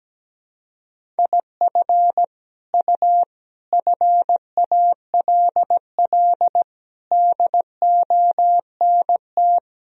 Morse code, heard as a keyed tone, about 17 wpm, 700 Hz. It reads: IF U FALL DONT
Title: Sentences from Top 500 Words and Abbreviations 17wpm